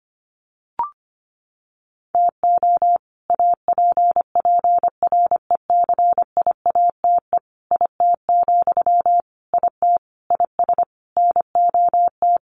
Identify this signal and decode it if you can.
Morse code, heard as a keyed tone, about 25 wpm, 700 Hz. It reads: TO APPRECIATE IT, IT IS NOT
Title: Sentences from Top 1300 Words 25wpm